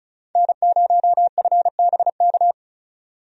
Morse code, heard as a keyed tone, about 36 wpm, 700 Hz. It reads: N0FBK
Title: USA Call Signs - 5 Characters - 35wpm